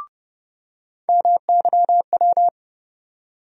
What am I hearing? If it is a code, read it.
Morse code, heard as a keyed tone, about 30 wpm, 700 Hz. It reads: MYW
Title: USA Call Sign Postfixes - 3 Letters 30wpm